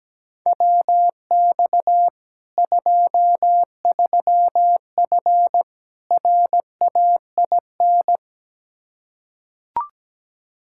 Morse code, heard as a keyed tone, about 17 wpm, 700 Hz. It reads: WX 23F RAIN
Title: Practice QSO - Exchange 2 - Info - 17wpm